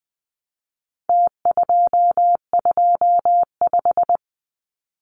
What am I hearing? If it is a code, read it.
Morse code, heard as a keyed tone, about 20 wpm, 700 Hz. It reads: T225
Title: Numbers - Contest Style - 4-Digit 20wpm